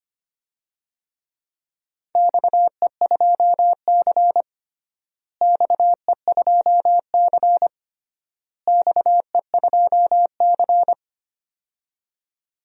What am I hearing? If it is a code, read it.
Morse code, heard as a keyed tone, about 25 wpm, 700 Hz. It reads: XE2C XE2C XE2C
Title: DXCC-100 Call Signs - 4 Characters - Repeated 3x - 25wpm